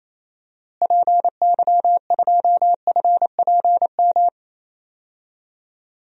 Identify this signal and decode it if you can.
Morse code, heard as a keyed tone, about 28 wpm, 700 Hz. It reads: PY2FPM
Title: DXCC-100 Call Signs - 6 Characters - Repeated 2x - 28wpm